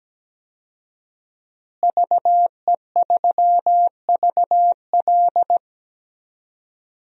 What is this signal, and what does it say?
Morse code, heard as a keyed tone, about 17 wpm, 700 Hz. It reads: VE3VL
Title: DXCC-100 Call Signs - 5 Characters - Repeated 2x - 17wpm